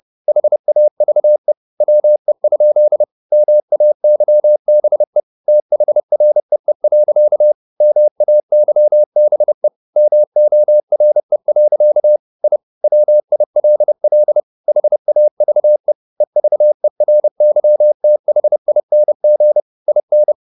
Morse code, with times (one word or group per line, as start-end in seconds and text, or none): 0.3-1.5 HAVE
1.8-3.0 WE?
3.3-5.2 MAYBE
5.5-7.5 THREE.
7.8-9.7 MAYBE
10.0-12.2 MORE.
12.4-12.6 I
12.8-14.4 WILL
14.7-15.9 HAVE
16.2-19.6 EVERYTHING
19.9-20.3 IN